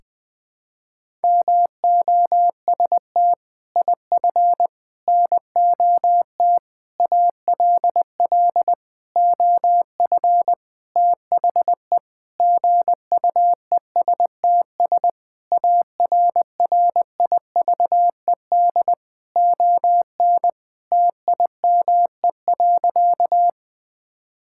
Morse code, heard as a keyed tone, about 20 wpm, 700 Hz. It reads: MOST IF NOT ALL OF THE GUESTS ARRIVED ON TIME.
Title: Sentences with 6-Word Phrases - Spoken Beginning - 20wpm